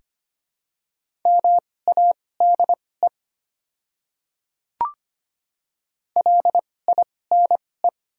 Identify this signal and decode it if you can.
Morse code, heard as a keyed tone, about 25 wpm, spaced as for 18 wpm, 690 Hz. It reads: MADE LINE
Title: 2x Character Spacing - Top 200 Words - 25wpm